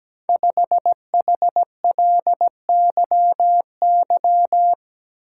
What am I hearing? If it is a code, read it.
Morse code, heard as a keyed tone, about 17 wpm, 700 Hz. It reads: 5HLYY
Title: Code Groups - 5 Letter:Numbers - 17wpm